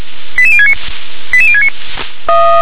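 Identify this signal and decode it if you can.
Five-tone selective call (ZVEI), heard as quick runs of five tones, tones of about 70 ms. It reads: 88069 88069
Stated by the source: ZVEI five-tone call 88060 received by SDR.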